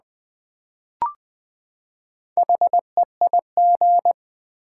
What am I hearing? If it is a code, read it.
Morse code, heard as a keyed tone, about 20 wpm, 700 Hz. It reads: HEIG
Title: Build A Word - 4 Letters At A Time - 8-Letter Words - 20wpm